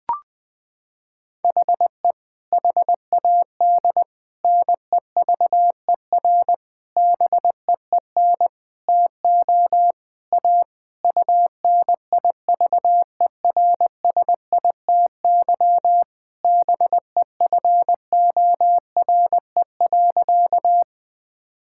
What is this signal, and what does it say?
Morse code, heard as a keyed tone, about 20 wpm, 700 Hz. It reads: HE HAD NEVER BEEN TO A UNIVERSITY BEFORE.
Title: Sentences from Top 1000 Words - Review Part 5 - 20wpm